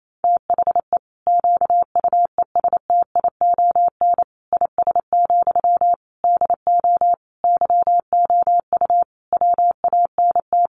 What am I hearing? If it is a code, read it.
Morse code, heard as a keyed tone, about 28 wpm, 700 Hz. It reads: THE QUESTION IS, DO YOU WANT
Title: Sentences from Top 1000 Words 28wpm